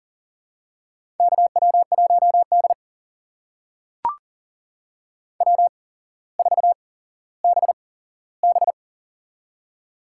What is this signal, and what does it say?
Morse code, heard as a keyed tone, about 40 wpm, spaced as for 17 wpm, 700 Hz. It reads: KW1D W4BB
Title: USA Call Signs - 4 Characters - 8x Character Spacing - 40wpm